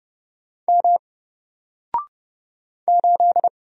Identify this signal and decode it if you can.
Morse code, heard as a keyed tone, about 30 wpm, 700 Hz. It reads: M 8
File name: Single-Letter-Number-30wpm